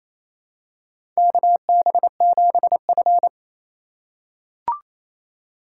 Morse code, heard as a keyed tone, about 28 wpm, 700 Hz. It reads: KB7F
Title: USA Call Signs - 2x Character Spacing - 28wpm